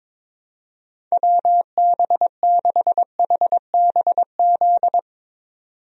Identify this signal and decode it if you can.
Morse code, heard as a keyed tone, about 22 wpm, 700 Hz. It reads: WB6HBZ
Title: USA Call Signs - 6 Characters - 22wpm